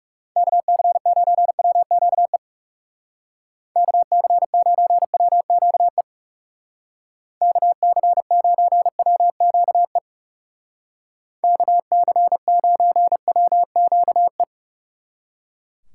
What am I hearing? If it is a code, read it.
Morse code, heard as a keyed tone, about 37 wpm, 700 Hz. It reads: KC9WQE KC9WQE KC9WQE KC9WQE
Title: Speed Racing - USA 2x3 Call Signs 30wpm